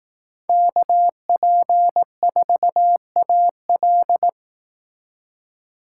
Morse code, heard as a keyed tone, about 18 wpm, 700 Hz. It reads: KP4AL
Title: Speed Racing - USA Call Signs 15wpm